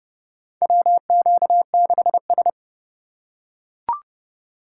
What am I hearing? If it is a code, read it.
Morse code, heard as a keyed tone, about 30 wpm, 700 Hz. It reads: WQ6S
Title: USA Call Signs - 2x Character Spacing - 30wpm